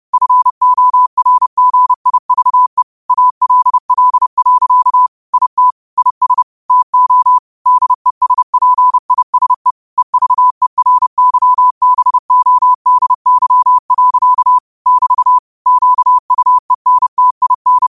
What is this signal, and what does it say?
Morse code, heard as a keyed tone, about 30 wpm, 1000 Hz. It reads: FORGIVE ALL. IT IS TO DESPISE EVERYBODY. <BT> QUENTIN
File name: qotdfiles / 30wpm